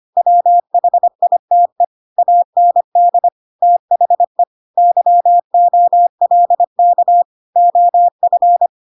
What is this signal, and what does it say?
Morse code, heard as a keyed tone, about 25 wpm, 700 Hz. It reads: WHITE AND THE YOLK OF
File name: Pinocchio / Pinocchio ch5 25 25 ik1hge